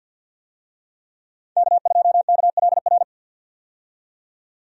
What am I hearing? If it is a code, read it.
Morse code, heard as a keyed tone, about 51 wpm, 700 Hz. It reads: K2KLR
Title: USA Call Signs - 5 Characters - 50wpm